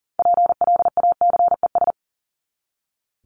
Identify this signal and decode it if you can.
Morse code, heard as a keyed tone, about 40 wpm, 700 Hz. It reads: PLACES
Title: ES:S Suffix - Top 100 Words with Base Words 40wpm